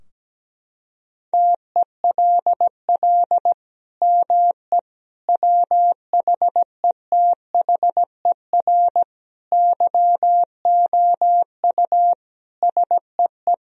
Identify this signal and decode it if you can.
Morse code, heard as a keyed tone, about 17 wpm, 700 Hz. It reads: TELL ME WHETHER YOU SEE
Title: Sentences from Top 1000 Words - Review Part 3 - 17wpm